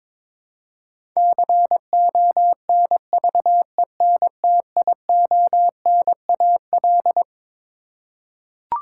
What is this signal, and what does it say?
Morse code, heard as a keyed tone, about 22 wpm, 700 Hz. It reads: CONVENTIONAL
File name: Twelve Letter Words - 22wpm